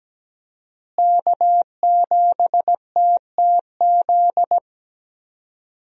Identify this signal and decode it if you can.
Morse code, heard as a keyed tone, about 17 wpm, 700 Hz. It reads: K7TTZ
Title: USA Call Signs - 5 Characters - Repeated 3x - 17wpm